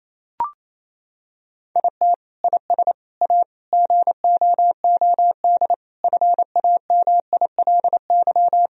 Morse code, heard as a keyed tone, about 28 wpm, 700 Hz. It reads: IT IS A GOOD FAMILY
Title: Sentences from Top 1000 Words - Review Part 2 - 28wpm